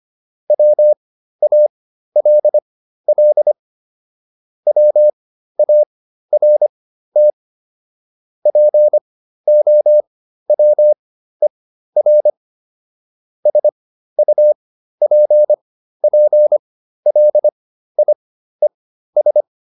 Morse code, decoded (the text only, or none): WALL WART POWER SUPPLIES